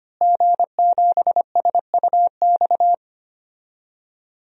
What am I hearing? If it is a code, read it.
Morse code, heard as a keyed tone, about 25 wpm, 700 Hz. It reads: G7SUX
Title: DXCC-100 Call Signs - 5 Characters - Repeated 2x - 25wpm